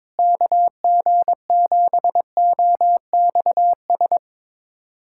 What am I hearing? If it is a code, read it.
Morse code, heard as a keyed tone, about 22 wpm, 700 Hz. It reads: KG7OXS
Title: USA Callsign DE Callsign 22wpm